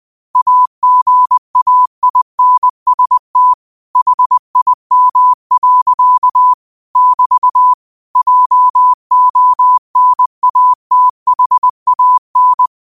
Morse code, heard as a keyed tone, about 20 wpm, 1000 Hz. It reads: AGAINST HIM. <BT> JONATHAN
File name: qotdfiles / 20wpm